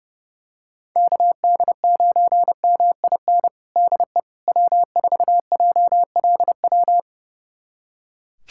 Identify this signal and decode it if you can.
Morse code, heard as a keyed tone, about 32 wpm, 700 Hz. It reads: KD9MIN DE W4JLW
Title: USA Callsign DE Callsign 30wpm